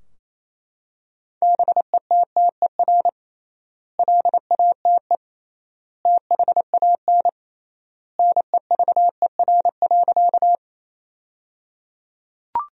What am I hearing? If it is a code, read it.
Morse code, heard as a keyed tone, about 28 wpm, 700 Hz. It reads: BETTER LATE THAN NEVER.
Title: 3x Word-Spacing - Sentences from Top 400 Words - 28wpm